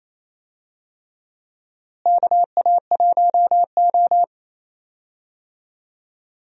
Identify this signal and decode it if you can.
Morse code, heard as a keyed tone, about 28 wpm, 700 Hz. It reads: KA1O
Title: USA Call Signs - 4 Characters - Repeated 3x - 28wpm